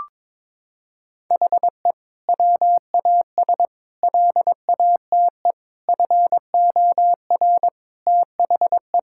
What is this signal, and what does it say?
Morse code, heard as a keyed tone, about 22 wpm, 700 Hz. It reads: HE WAS LATE FOR THE
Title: Sentences from Top 1600 Words 22wpm